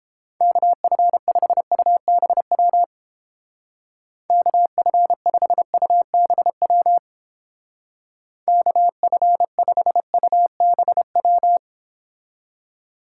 Morse code, heard as a keyed tone, about 29 wpm, 700 Hz. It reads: KF5UBW KF5UBW KF5UBW